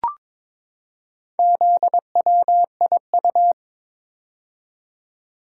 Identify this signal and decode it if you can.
Morse code, heard as a keyed tone, about 21 wpm, 700 Hz. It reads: ZWIU
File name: practice / 22wpm / Four Letter-Number Combinations - 22wpm